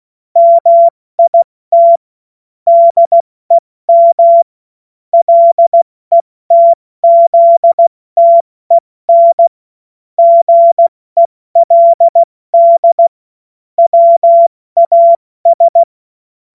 Morse code, heard as a keyed tone, about 16 wpm, 680 Hz. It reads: MIT DEM LETZTEN GELD WAS
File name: Morsekurs / Witze / Witz 22. Februar 2022 70 F80